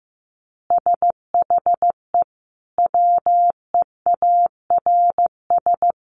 Morse code, heard as a keyed tone, about 15 wpm, 700 Hz. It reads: SHE WEARS
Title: Additional Easy Phrases 15wpm